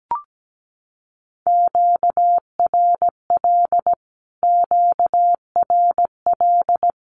Morse code, heard as a keyed tone, about 17 wpm, 700 Hz. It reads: QRL QRL
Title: Q-Codes Repeated 3x 17wpm